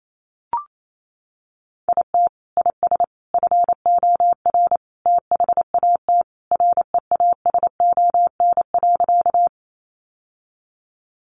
Morse code, heard as a keyed tone, about 28 wpm, 700 Hz. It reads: IT IS FOR THAT REASON.
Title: Sentences from Top 700 Words 28wpm